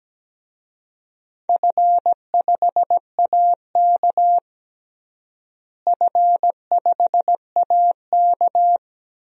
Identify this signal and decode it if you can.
Morse code, heard as a keyed tone, about 17 wpm, 700 Hz. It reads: F5AK F5AK
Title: DXCC-100 Call Signs - 4 Characters - Repeated 2x - 17wpm